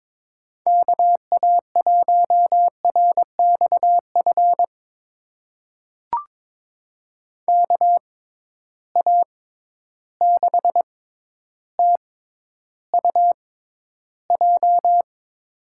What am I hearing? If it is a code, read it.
Morse code, heard as a keyed tone, about 23 wpm, spaced as for 11 wpm, 700 Hz. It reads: KA1RXF KA6TUJ
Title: USA Call Signs - 6 Characters - 6x Character Spacing - 22wpm